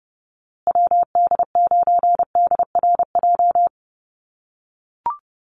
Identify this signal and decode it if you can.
Morse code, heard as a keyed tone, about 30 wpm, 700 Hz. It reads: WD9DRJ
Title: USA Call Signs Repeated 3x 30wpm